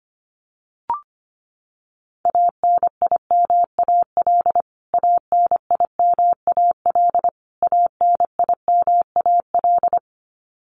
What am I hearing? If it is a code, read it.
Morse code, heard as a keyed tone, about 25 wpm, 700 Hz. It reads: ANIMAL ANIMAL ANIMAL
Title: Top 300 Words Repeated 25wpm